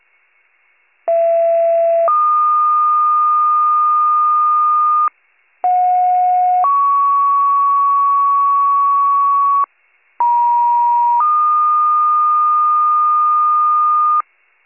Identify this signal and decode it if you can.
Two-tone page, heard as a one-second tone, then a three-second tone, about 670 then 1220 Hz, about 730 then 1120 Hz, about 950 then 1220 Hz. It is heard from 1.1 to 14.2 s.